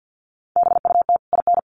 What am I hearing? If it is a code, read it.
Morse code, heard as a keyed tone, about 48 wpm, 700 Hz. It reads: BUT IN
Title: Sentences from Top 1500 Words 50wpm